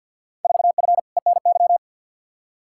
Morse code, heard as a keyed tone, about 52 wpm, 700 Hz. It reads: 4F ENY
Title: Field Day Exchanges - 50wpm